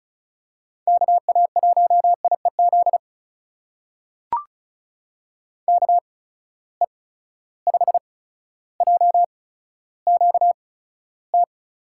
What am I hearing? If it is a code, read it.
Morse code, heard as a keyed tone, about 36 wpm, spaced as for 15 wpm, 700 Hz. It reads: KA1IEZ KE5JQT
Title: USA Call Signs - 6 Characters - 8x Character Spacing - 35wpm